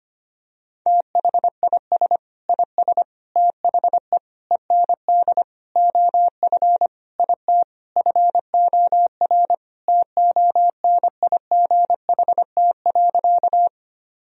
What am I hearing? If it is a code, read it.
Morse code, heard as a keyed tone, about 25 wpm, 700 Hz. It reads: THIS IS THE END OF IT FOR TONIGHT.
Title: Sentences from Top 1000 Words - Review Part 3 - 25wpm